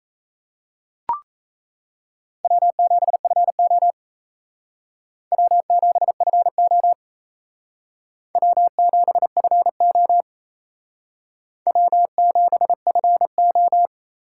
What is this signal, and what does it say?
Morse code, heard as a keyed tone, about 37 wpm, 700 Hz. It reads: W7FO W7FO W7FO W7FO